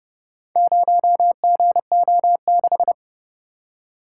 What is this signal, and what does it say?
Morse code, heard as a keyed tone, about 31 wpm, 700 Hz. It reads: 0GO6